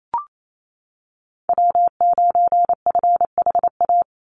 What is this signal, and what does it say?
Morse code, heard as a keyed tone, about 28 wpm, 700 Hz. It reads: W9FHA
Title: NA SOTA Chaser Callsigns 28wpm